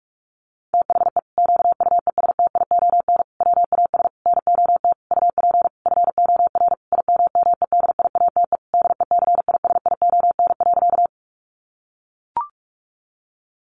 Morse code, heard as a keyed tone, about 44 wpm, 700 Hz. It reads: THE QUESTION WAS NOT UP FOR IMMEDIATE DECISION.
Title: Sentences from Top 1200 Words 45wpm